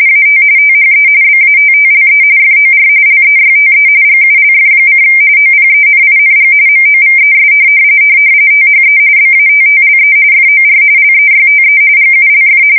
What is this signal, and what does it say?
RTTY, about 45.45 baud. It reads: AT THE BATTLE FOR LEYTE GULF: OPENING MOVES THE BATTLE FOR LEYTE GULF
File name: fsk170 / USS-New-Jersey